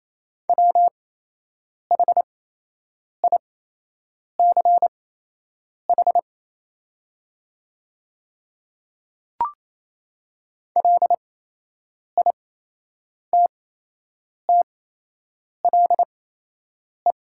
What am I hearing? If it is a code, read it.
Morse code, heard as a keyed tone, about 28 wpm, spaced as for 8 wpm, 700 Hz. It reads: WHICH LITTLE
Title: 8x Character Spacing - Top 200 Words - 28wpm